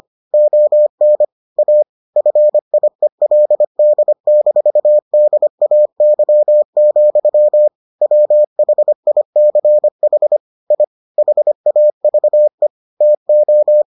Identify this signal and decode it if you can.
Morse code, heard as a keyed tone, about 25 wpm, 600 Hz. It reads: ON A FIELD-DAY, WHICH I HAVE TO